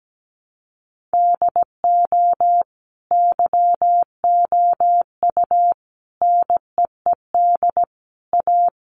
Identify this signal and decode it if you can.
Morse code, heard as a keyed tone, about 17 wpm, 700 Hz. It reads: DO YOU NEED A
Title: Sentences from Top 1500 Words 17wpm